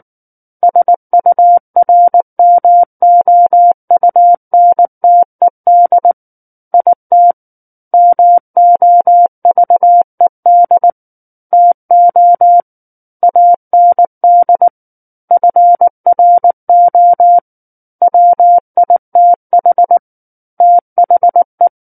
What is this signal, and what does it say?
Morse code, heard as a keyed tone, about 19 wpm, 700 Hz. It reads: SURMOUNTED IT MOVED TO AND FRO WITH THE
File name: Book One / 10-Chapter 10 - 19 WPM